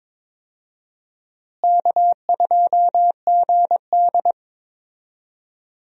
Morse code, heard as a keyed tone, about 22 wpm, 700 Hz. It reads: K2GD
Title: USA Call Signs - 4 Characters - Repeated 2x - 22wpm